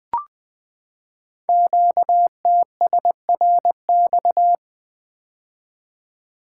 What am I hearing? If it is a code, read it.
Morse code, heard as a keyed tone, about 20 wpm, 700 Hz. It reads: QTSRX
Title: Code Groups - 5 Letters - 20wpm